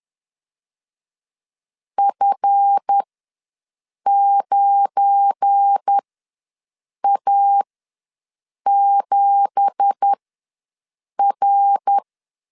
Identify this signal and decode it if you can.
Morse code, heard as a keyed tone, about 10 wpm, spaced as for 6 wpm, 800 Hz. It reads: F9A7R
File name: LEZIONI / 10 SIGNS / 10B INT ws 10wpm 4134